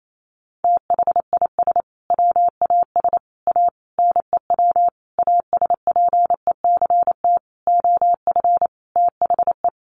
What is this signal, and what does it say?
Morse code, heard as a keyed tone, about 28 wpm, 700 Hz. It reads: THIS WAS A NEW ASPECT OF THE
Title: Sentences from Top 1300 Words 28wpm